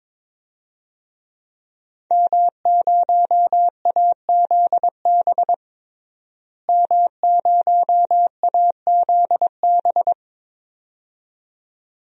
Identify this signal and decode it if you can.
Morse code, heard as a keyed tone, about 22 wpm, 700 Hz. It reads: M0AZB M0AZB
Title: DXCC-100 Call Signs - 5 Characters - Repeated 2x - 22wpm